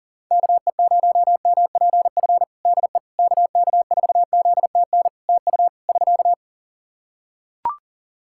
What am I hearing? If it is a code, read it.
Morse code, heard as a keyed tone, about 38 wpm, 700 Hz. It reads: KE0MPF DE KK4ZTN TU <SK>
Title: Practice QSO - Exchange 4 - Ending - 40wpm